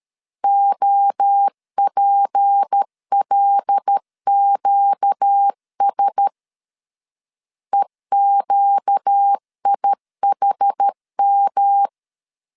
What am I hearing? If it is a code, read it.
Morse code, heard as a keyed tone, about 13 wpm, 800 Hz. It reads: OPLQS EQIHM
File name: LEZIONI / 6 PKX0 / 6 P 12wpm 2033